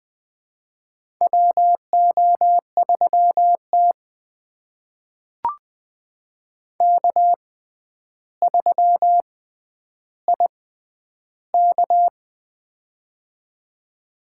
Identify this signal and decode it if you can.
Morse code, heard as a keyed tone, about 20 wpm, spaced as for 10 wpm, 700 Hz. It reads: WO3T K3IK
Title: USA Call Signs - 4 Characters - 6x Character Spacing - 20wpm